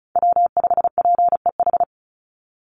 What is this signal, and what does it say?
Morse code, heard as a keyed tone, about 34 wpm, 700 Hz. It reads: W5PEH
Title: Call Signs Repeated 3x - 35wpm